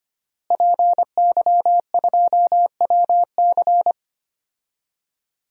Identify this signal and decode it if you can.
Morse code, heard as a keyed tone, about 25 wpm, 700 Hz. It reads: PY2WC
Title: DXCC-100 Call Signs - 5 Characters - Repeated 2x - 25wpm